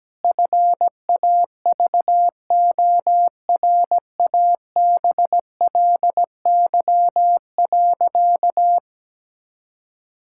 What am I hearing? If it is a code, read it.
Morse code, heard as a keyed tone, about 17 wpm, 700 Hz. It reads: FAVORABLY.
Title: Pangram Challenge v2 - 17wpm